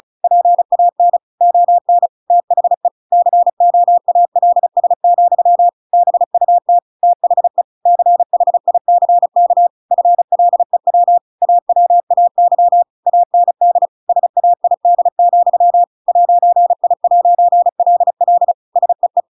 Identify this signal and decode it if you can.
Morse code, heard as a keyed tone, about 34 wpm, 700 Hz. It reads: PAN ON THE COALS, BUT THE CHICK FLEW AWAY AND SAID, 'I'LL SEE
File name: Pinocchio / Pinocchio ch7 35 35 ik1hge